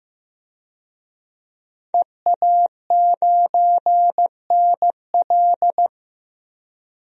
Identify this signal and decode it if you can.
Morse code, heard as a keyed tone, about 15 wpm, 700 Hz. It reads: EA9NL
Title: DXCC-100 Call Signs - 5 Characters - Repeated 3x - 15wpm